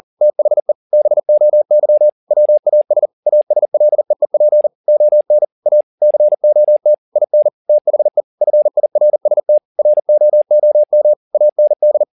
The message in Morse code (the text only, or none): THE BOY WAS ASLEEP ON A COT IN THE FIRST ROOM AND